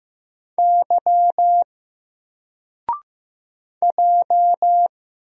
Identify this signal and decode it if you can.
Morse code, heard as a keyed tone, about 15 wpm, 700 Hz. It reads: Y J
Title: Single Letter-Number Alt 15wpm